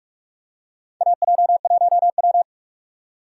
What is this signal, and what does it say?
Morse code, heard as a keyed tone, about 44 wpm, 700 Hz. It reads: AJ1W
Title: USA Call Signs - 4 Characters - 4x Character Spacing - 45wpm